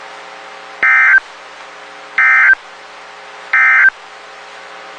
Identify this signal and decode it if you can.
Emergency Alert System (SAME) tones, heard as harsh buzzing bursts. It is heard from 0.7 to 4.0 s.
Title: WXK91 NOAA Weather Radio